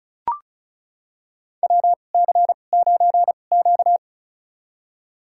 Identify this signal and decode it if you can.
Morse code, heard as a keyed tone, about 36 wpm, spaced as for 25 wpm, 700 Hz. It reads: WC9Q
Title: USA Call Signs - 4 Characters - 2x Character Spacing - 35wpm